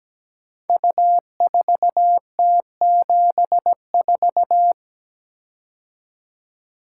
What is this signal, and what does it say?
Morse code, heard as a keyed tone, about 17 wpm, 700 Hz. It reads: U4T74
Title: Code Groups - 5 Letter:Numbers - 17wpm